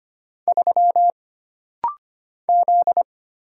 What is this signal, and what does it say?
Morse code, heard as a keyed tone, about 25 wpm, 700 Hz. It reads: 3 Z
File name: Single Character (Letter:Number:Punctuation:Prosign) - Rapid-Fire - 25wpm